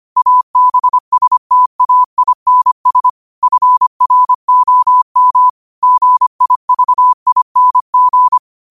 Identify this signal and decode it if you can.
Morse code, heard as a keyed tone, about 25 wpm, 1000 Hz. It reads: ABSTAINS FROM GIVING
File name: qotdfiles / 25wpm